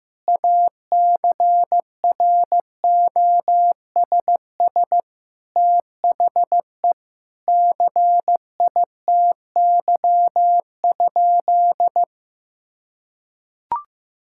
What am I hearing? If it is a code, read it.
Morse code, heard as a keyed tone, about 15 wpm, 700 Hz. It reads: ACROSS THE CITY?
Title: Sentences from Top 1000 Words - Review Part 5 - 15wpm